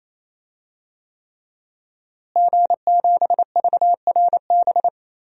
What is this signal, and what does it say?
Morse code, heard as a keyed tone, about 28 wpm, 700 Hz. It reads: G7VRB
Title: DXCC-100 Call Signs - 5 Characters - Repeated 3x - 28wpm